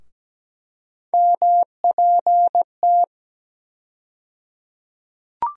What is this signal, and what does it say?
Morse code, heard as a keyed tone, about 17 wpm, 700 Hz. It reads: MPT
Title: USA Call Sign Postfixes - 3 Letters 17wpm